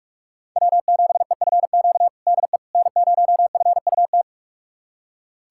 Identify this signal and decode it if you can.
Morse code, heard as a keyed tone, about 43 wpm, 700 Hz. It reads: W7EFQ DE N0FUT
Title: USA Callsign DE Callsign - 5 Characters - 45wpm